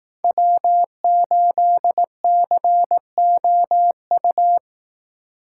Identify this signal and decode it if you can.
Morse code, heard as a keyed tone, about 18 wpm, 700 Hz. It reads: W8COU
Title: Speed Racing - USA 1x3 Call Signs 15wpm